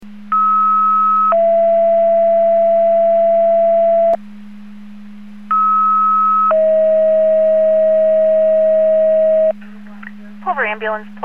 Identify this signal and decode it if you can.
Two-tone page, heard as a one-second tone, then a three-second tone, about 1290 then 670 Hz, about 1290 then 640 Hz, with speech. It is heard from 0.3 to 9.5 s.